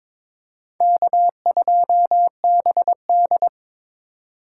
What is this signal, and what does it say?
Morse code, heard as a keyed tone, about 23 wpm, 700 Hz. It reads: K2BD
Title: USA Call Signs - 4 Characters - 3x Character Spacing - 22wpm